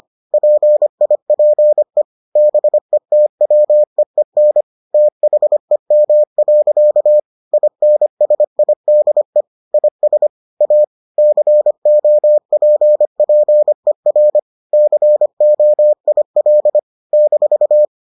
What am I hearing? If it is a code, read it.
Morse code, heard as a keyed tone, about 25 wpm, 600 Hz. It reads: PIPE BETWEEN THEM. INSIDE IS A COPPER COIL -